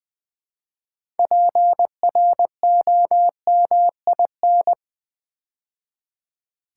Morse code, heard as a keyed tone, about 20 wpm, 700 Hz. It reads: PROMIN
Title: Build A Word - 3 Letters At A Time - 9-Letter Words - 20wpm